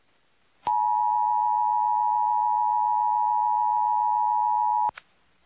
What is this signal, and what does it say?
Emergency Alert System (SAME) tones (the 853 and 960 Hz attention tone). It is heard from 0.6 to 5.0 s.